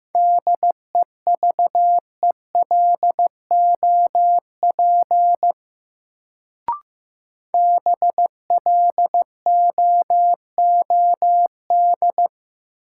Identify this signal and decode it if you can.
Morse code, heard as a keyed tone, about 15 wpm, 700 Hz. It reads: DEVELOP BLOOD
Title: Sentences from Top 1000 Words - Review Part 2 - New Word Drill - 15wpm